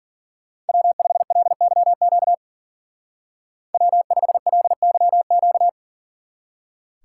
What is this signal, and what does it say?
Morse code, heard as a keyed tone, about 48 wpm, 700 Hz. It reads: W5LYQ W5LYQ
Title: Speed Racing - USA Call Signs 40wpm